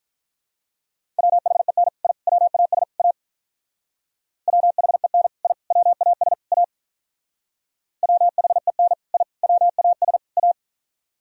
Morse code, heard as a keyed tone, about 44 wpm, 700 Hz. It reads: WHEN I WAS A WHEN I WAS A WHEN I WAS A
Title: Speed-Racing - Sets of 4 Words 35wpm